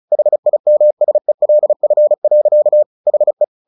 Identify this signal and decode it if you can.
Morse code, heard as a keyed tone, about 34 wpm, 600 Hz. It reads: HIMSELF. HE
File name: the old man and the sea / the old man and the sea 1 35wpm